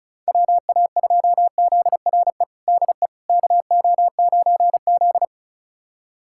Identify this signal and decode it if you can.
Morse code, heard as a keyed tone, about 36 wpm, 700 Hz. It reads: WA2ZRE DE KO9Z
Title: USA Callsign DE Callsign 35wpm